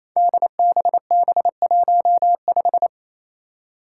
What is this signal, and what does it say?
Morse code, heard as a keyed tone, about 28 wpm, 700 Hz. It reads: DBB15
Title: Five Challenging Letter-Number Combinations 28wpm